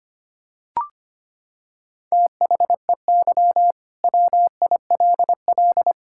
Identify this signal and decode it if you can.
Morse code, heard as a keyed tone, about 25 wpm, 700 Hz. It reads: THEY WILL
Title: Sentences from Top 1600 Words 25wpm